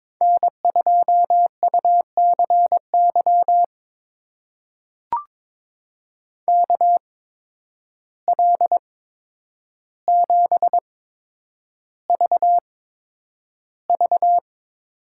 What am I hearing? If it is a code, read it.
Morse code, heard as a keyed tone, about 22 wpm, spaced as for 9 wpm, 700 Hz. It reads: N2UCY KL7VV
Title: USA Call Signs - 5 Characters - 8x Character Spacing - 22wpm